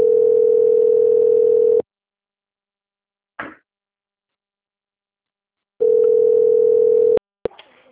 Ringback tone (North America), 440 + 480 Hz.
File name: rg-1600-unknown- - - .26693